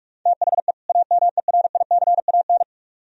Morse code, heard as a keyed tone, about 53 wpm, 700 Hz. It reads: THE AMERICAN
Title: Speed-Racing - Sets of 2 Words 45wpm